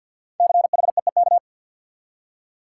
Morse code, heard as a keyed tone, about 49 wpm, 700 Hz. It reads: CHEEK
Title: Five Letter Words - 50wpm